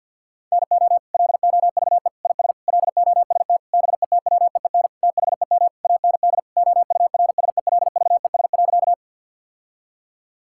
Morse code, heard as a keyed tone, about 50 wpm, 700 Hz. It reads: NO LOVE IS LOST BETWEEN THEM AND OURSELVES.
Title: Sentences from Top 1600 Words 50wpm